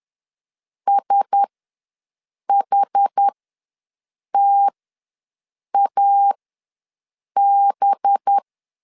Morse code, heard as a keyed tone, about 10 wpm, spaced as for 6 wpm, 800 Hz. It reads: SHTAB
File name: LEZIONI / 5 LBVC / 5 LBV ws 10wpm 3645